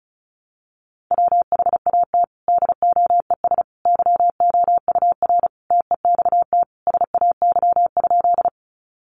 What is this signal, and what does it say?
Morse code, heard as a keyed tone, about 34 wpm, 700 Hz. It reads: WHAT DOES YOUR TEXT SAY?
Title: Sentences from Top 1200 Words 35wpm